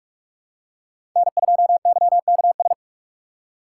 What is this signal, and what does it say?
Morse code, heard as a keyed tone, about 46 wpm, 700 Hz. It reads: N2YKS
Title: USA Call Signs - 5 Characters - 2x Character Spacing - 45wpm